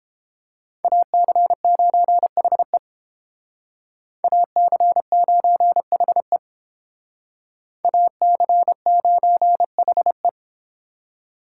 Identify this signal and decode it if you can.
Morse code, heard as a keyed tone, about 31 wpm, 700 Hz. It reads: AC9HE AC9HE AC9HE